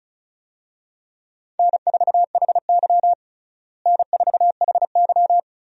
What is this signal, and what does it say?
Morse code, heard as a keyed tone, about 36 wpm, 700 Hz. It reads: N4HY N4HY
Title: USA Call Signs - 4 Characters - Repeated 2x - 35wpm